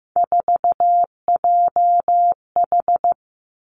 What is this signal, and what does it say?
Morse code, heard as a keyed tone, about 15 wpm, 700 Hz. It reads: 4JH
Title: Three Challenging Letter-Number Combinations 15wpm